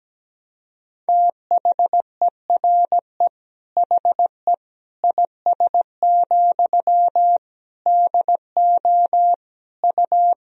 Morse code, heard as a keyed tone, about 17 wpm, 700 Hz. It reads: THERE HE IS, DO U
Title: Sentences from Top 100 Words and Abbreviations 17wpm